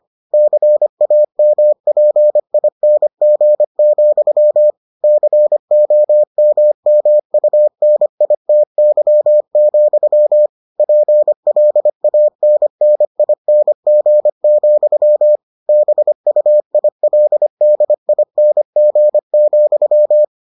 Morse code, heard as a keyed tone, about 25 wpm, 600 Hz. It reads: CAMPING, COMMUNITY, PLANNING, BUILDING,